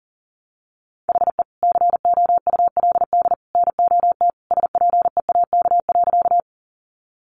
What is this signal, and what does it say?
Morse code, heard as a keyed tone, about 40 wpm, 700 Hz. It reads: HE COULD NOT SPEAK.
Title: Sentences from Top 1100 Words 40wpm